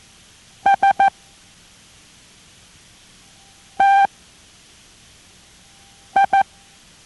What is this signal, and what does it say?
Morse code, heard as a keyed tone, about 14 wpm, 790 Hz. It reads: S T I